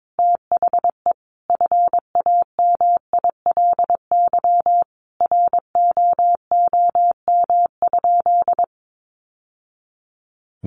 Morse code, heard as a keyed tone, about 22 wpm, 700 Hz. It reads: THE FAMILY ROOM?
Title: Sentences from Top 1200 Words 22wpm